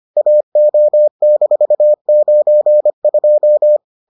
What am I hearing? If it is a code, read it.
Morse code, heard as a keyed tone, about 25 wpm, 600 Hz. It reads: AO-92